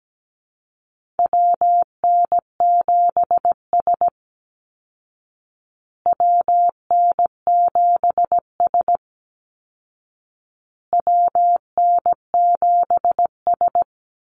Morse code, heard as a keyed tone, about 17 wpm, 700 Hz. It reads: WN7S WN7S WN7S
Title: Call Signs Repeated 2x - 17wpm